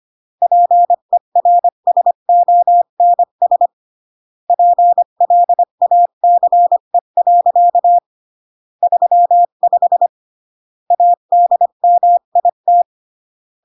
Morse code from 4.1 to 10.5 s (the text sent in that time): PLACE. 35